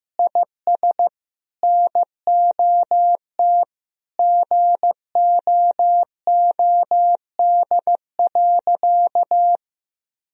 Morse code, heard as a keyed tone, about 15 wpm, 700 Hz. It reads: IS NOT GOOD.
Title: Sentences from Top 1000 Words - Review Part 1 - 15wpm